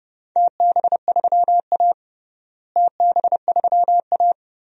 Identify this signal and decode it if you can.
Morse code, heard as a keyed tone, about 30 wpm, 700 Hz. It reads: TB3A TB3A
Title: DXCC-100 Call Signs - 4 Characters - Repeated 2x - 30wpm